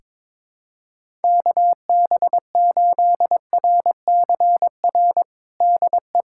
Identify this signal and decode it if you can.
Morse code, heard as a keyed tone, about 22 wpm, 700 Hz. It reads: KB8RCR DE
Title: USA Callsign DE Callsign - 6 Characters - 22wpm